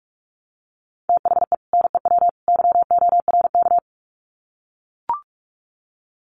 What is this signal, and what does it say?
Morse code, heard as a keyed tone, about 44 wpm, 700 Hz. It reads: THE NEW YORK
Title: Sets of 3 Words 45wpm